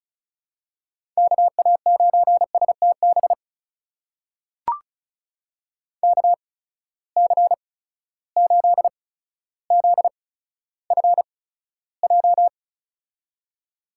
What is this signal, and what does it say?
Morse code, heard as a keyed tone, about 36 wpm, spaced as for 15 wpm, 700 Hz. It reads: KA9STB KC8ZFJ
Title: USA Call Signs - 6 Characters - 8x Character Spacing - 35wpm